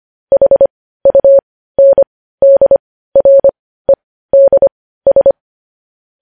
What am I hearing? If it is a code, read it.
Morse code, heard as a keyed tone, about 25 wpm, spaced as for 15 wpm, 550 Hz. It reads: HUNDREDS
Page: News headlines in Morse code at 15 wpm